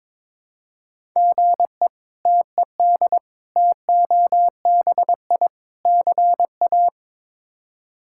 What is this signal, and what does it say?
Morse code, heard as a keyed tone, about 22 wpm, 700 Hz. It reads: GE TED TOBI CA
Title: K1USN SST QSO Practice - 22wpm